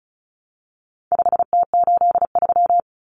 Morse code, heard as a keyed tone, about 34 wpm, 700 Hz. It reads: 5T83
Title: Common Cut Numbers - 4-Digit 35wpm